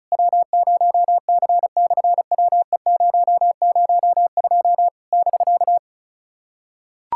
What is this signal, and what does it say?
Morse code, heard as a keyed tone, about 33 wpm, 700 Hz. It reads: W0C/WE002 <BK>
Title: SOTA QSO Practice 35wpm